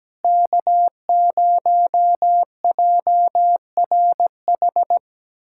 Morse code, heard as a keyed tone, about 17 wpm, 700 Hz. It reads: K0JRH
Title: USA Callsign DE Callsign 17wpm